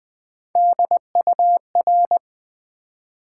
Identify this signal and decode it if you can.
Morse code, heard as a keyed tone, about 20 wpm, 690 Hz. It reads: DUR
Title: Build A Word - 3 Letters At A Time - 8-Letter Words - 20wpm